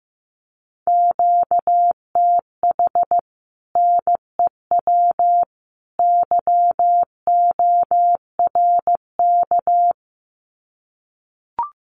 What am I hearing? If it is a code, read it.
Morse code, heard as a keyed tone, about 15 wpm, 700 Hz. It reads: QTH NEW YORK
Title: Common QSO Phrases 15wpm